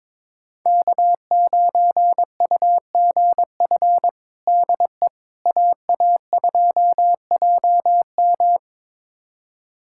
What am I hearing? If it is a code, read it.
Morse code, heard as a keyed tone, about 22 wpm, 700 Hz. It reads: K9UGF DE AA2JM
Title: USA Callsign DE Callsign - 5 Characters - 22wpm